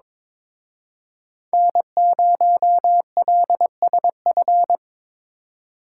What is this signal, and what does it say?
Morse code, heard as a keyed tone, about 26 wpm, 700 Hz. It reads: N0LSF
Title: Speed Racing - USA 1x3 Call Signs 22wpm